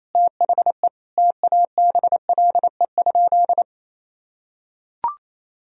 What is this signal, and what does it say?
Morse code, heard as a keyed tone, about 28 wpm, 700 Hz. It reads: THE TABLE?
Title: Sentences from Top 1000 Words - Review Part 3 - 28wpm